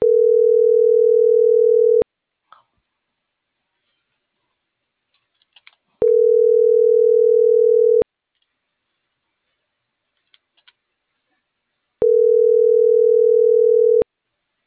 Ringback tone (North America), 440 + 480 Hz.